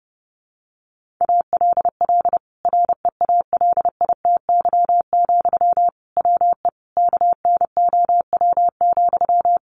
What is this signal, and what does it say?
Morse code, heard as a keyed tone, about 30 wpm, 700 Hz. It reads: ALL REALITY, WE KNOW,
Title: Sentences from Top 1400 Words 30wpm